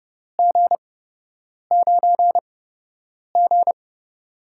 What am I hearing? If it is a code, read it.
Morse code, heard as a keyed tone, about 29 wpm, spaced as for 8 wpm, 700 Hz. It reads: G9G
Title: USA Call Signs - 4 Characters - 8x Character Spacing - 30wpm